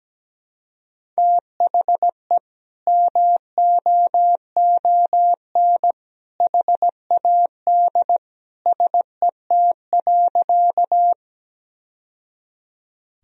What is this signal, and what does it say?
Morse code, heard as a keyed tone, about 17 wpm, 700 Hz. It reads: THE MOON HAD SET.
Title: Sentences from Top 1000 Words - Review Part 3 - 17wpm